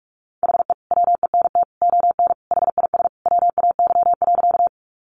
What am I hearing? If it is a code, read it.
Morse code, heard as a keyed tone, about 44 wpm, 700 Hz. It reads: HE WENT ON HIS WAY.
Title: Sentences from Top 200 Words 45wpm